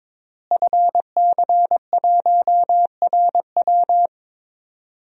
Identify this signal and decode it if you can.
Morse code, heard as a keyed tone, about 22 wpm, 700 Hz. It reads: FC1RW
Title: DXCC-100 Call Signs - 5 Characters - Repeated 3x - 22wpm